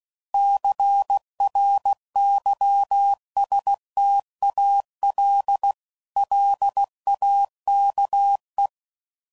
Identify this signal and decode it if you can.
Morse code, heard as a keyed tone, about 16 wpm, 800 Hz. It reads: CRYSTAL LAKE
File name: CW Code Groups @15WPM